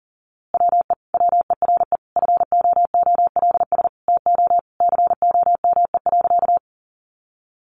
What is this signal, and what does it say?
Morse code, heard as a keyed tone, about 40 wpm, 700 Hz. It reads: WE WERE FOOLS TO COME.
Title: A Sound Of Thunder 40wpm